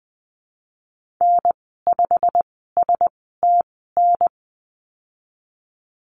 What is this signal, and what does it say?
Morse code, heard as a keyed tone, about 20 wpm, spaced as for 15 wpm, 700 Hz. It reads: N5STN
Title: 167 Lesson 27g - Number 7 - Call Signs 20wpm